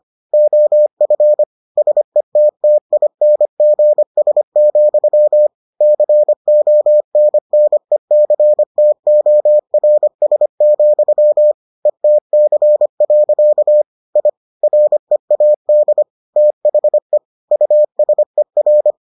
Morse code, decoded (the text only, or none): OF SETTINGS, CONNECTORS, ETC. I READ THE USER